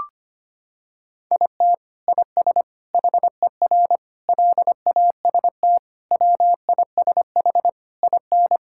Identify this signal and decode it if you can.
Morse code, heard as a keyed tone, about 25 wpm, 700 Hz. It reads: IT IS HER LAST WISH IN
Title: Sentences from Top 1600 Words 25wpm